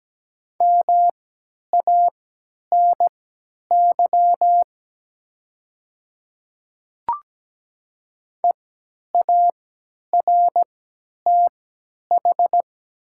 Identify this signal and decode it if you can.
Morse code, heard as a keyed tone, about 17 wpm, spaced as for 10 wpm, 700 Hz. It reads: MANY EARTH
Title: 3x Character Spacing - Top 200 Words - 17wpm